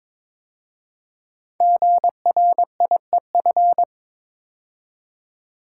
Morse code, heard as a keyed tone, about 22 wpm, 700 Hz. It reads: GRIEF
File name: 5 Letter Words - First 3 Letters Given - 22wpm